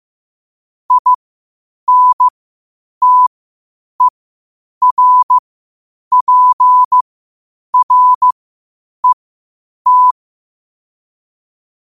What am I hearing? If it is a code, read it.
Morse code, heard as a keyed tone, about 15 wpm, spaced as for 8 wpm, 1000 Hz. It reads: INTERPRET
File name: qotdfiles / 8wpm